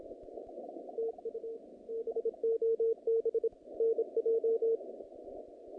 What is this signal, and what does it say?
Morse code, heard as a keyed tone, about 28 wpm, 470 Hz. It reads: TU BOB NJ
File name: contests / naqp-cw-k5tr-op-k5pi- -2300